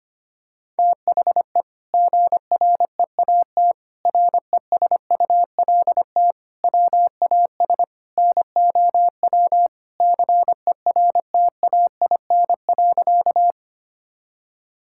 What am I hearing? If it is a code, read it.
Morse code, heard as a keyed tone, about 25 wpm, 700 Hz. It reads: THE GREAT RESULT WAS NOW CERTAIN.
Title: Sentences from Top 1000 Words - Review Part 5 - 25wpm